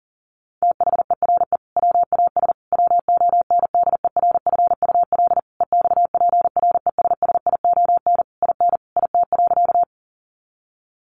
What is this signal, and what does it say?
Morse code, heard as a keyed tone, about 40 wpm, 700 Hz. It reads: THERE WAS WONDERFUL EXPRESSION IN IT.
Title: Sentences from Top 1200 Words 40wpm